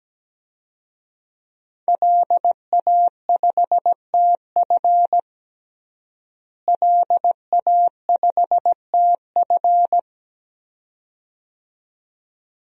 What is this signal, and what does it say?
Morse code, heard as a keyed tone, about 17 wpm, 700 Hz. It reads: LA5TF LA5TF
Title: DXCC-100 Call Signs - 5 Characters - Repeated 2x - 17wpm